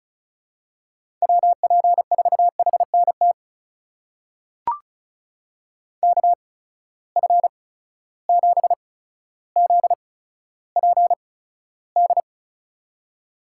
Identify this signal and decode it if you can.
Morse code, heard as a keyed tone, about 34 wpm, spaced as for 15 wpm, 700 Hz. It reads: WP4HNT KF7ZPD
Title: USA Call Signs - 6 Characters - 8x Character Spacing - 35wpm